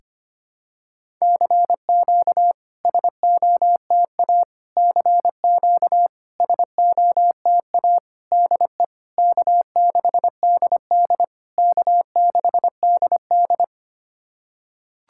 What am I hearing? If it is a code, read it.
Morse code, heard as a keyed tone, about 25 wpm, 700 Hz. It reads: CQ SOTA CQ SOTA DE K6DD K6DD
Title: SOTA QSO Practice 25wpm